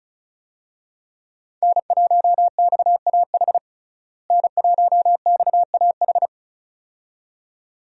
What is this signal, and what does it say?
Morse code, heard as a keyed tone, about 34 wpm, 700 Hz. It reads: N1XAH N1XAH
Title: USA Call Signs - 5 Characters - Repeated 2x - 35wpm